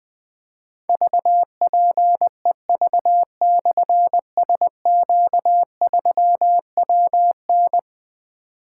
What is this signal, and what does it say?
Morse code, heard as a keyed tone, about 20 wpm, 700 Hz. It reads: VPEV/SQ3WN
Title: Difficult Call Signs 20wpm